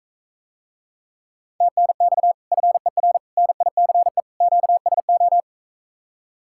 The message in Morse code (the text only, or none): TNX FER NICE QSO